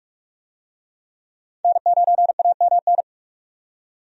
Morse code, heard as a keyed tone, about 46 wpm, 700 Hz. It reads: N9AMN
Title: USA Call Signs - 5 Characters - 45wpm